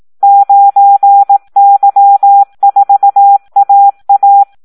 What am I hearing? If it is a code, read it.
Morse code, heard as a keyed tone, about 18 wpm, 800 Hz. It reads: 9Y4AA
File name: Audio practica CW / SD 1 18